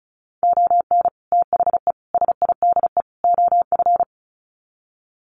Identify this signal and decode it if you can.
Morse code, heard as a keyed tone, about 34 wpm, 700 Hz. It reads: ON THE SIDE OF
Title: Sets of 4 Words 35wpm